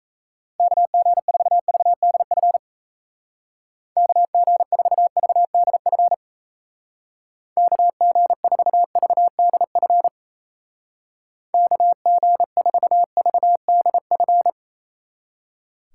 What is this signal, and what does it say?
Morse code, heard as a keyed tone, about 36 wpm, 700 Hz. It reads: KG4VDF KG4VDF KG4VDF KG4VDF